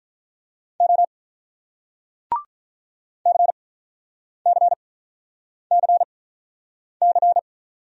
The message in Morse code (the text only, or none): O CCCC